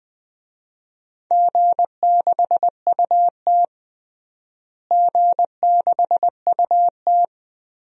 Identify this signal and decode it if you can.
Morse code, heard as a keyed tone, about 21 wpm, 700 Hz. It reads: G6UT G6UT
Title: DXCC-100 Call Signs - 4 Characters - Repeated 2x - 20wpm